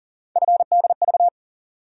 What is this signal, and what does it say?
Morse code, heard as a keyed tone, about 40 wpm, 700 Hz. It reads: FDV
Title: USA Call Sign Postfixes - 3 Letters 40wpm